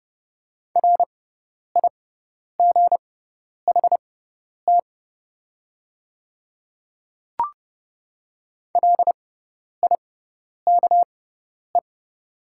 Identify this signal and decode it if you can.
Morse code, heard as a keyed tone, about 30 wpm, spaced as for 10 wpm, 700 Hz. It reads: RIGHT LIKE
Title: 6x Character Spacing - Top 200 Words - 30wpm